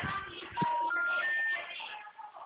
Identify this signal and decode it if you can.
Special information tone, with noise.
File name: out- -2006- - - .407